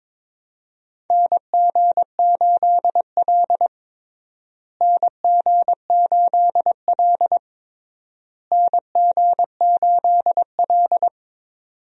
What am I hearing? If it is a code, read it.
Morse code, heard as a keyed tone, about 22 wpm, 700 Hz. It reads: NG8L NG8L NG8L
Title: USA Call Signs - 4 Characters - Repeated 3x - 22wpm